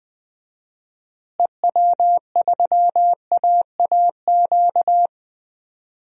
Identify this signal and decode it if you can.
Morse code, heard as a keyed tone, about 20 wpm, 700 Hz. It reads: EW3AAQ
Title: DXCC-100 Call Signs - 6 Characters - Repeated 2x - 20wpm